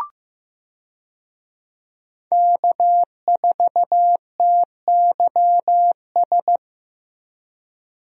Morse code, heard as a keyed tone, about 15 wpm, 700 Hz. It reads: K4TYS
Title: USA Call Signs - 5 Characters - Repeated 2x - 15wpm